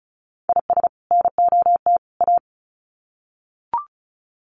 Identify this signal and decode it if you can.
Morse code, heard as a keyed tone, about 34 wpm, 700 Hz. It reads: IS NOT A
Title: Sets of 3 Words 35wpm